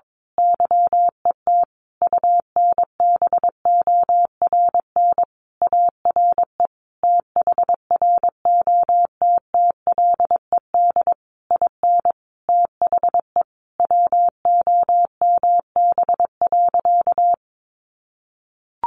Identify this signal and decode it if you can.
Morse code, heard as a keyed tone, about 22 wpm, 700 Hz. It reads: YET UNBORN ARE THROTTLED IN THE WOMB.
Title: A Sound Of Thunder 22wpm